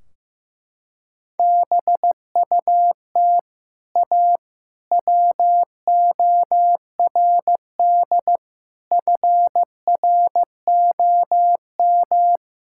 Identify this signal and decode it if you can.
Morse code, heard as a keyed tone, about 15 wpm, 700 Hz. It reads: BUT A WORD FROM
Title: Sentences from Top 1600 Words 15wpm